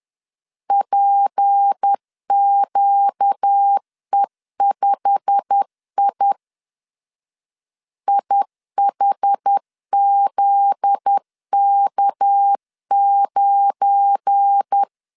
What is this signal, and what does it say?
Morse code, heard as a keyed tone, about 10 wpm, 800 Hz. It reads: PQE5I IHZK9
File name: CW / DET / 1 9 1500C S B 10wpm 4439